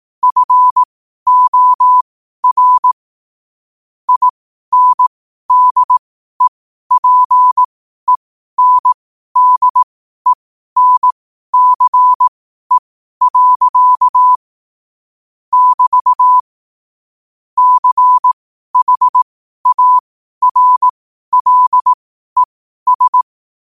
Morse code, heard as a keyed tone, about 18 wpm, spaced as for 12 wpm, 1000 Hz. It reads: FOR INDEPENDENCE. <BT> CHARLES
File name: qotdfiles / 12wpm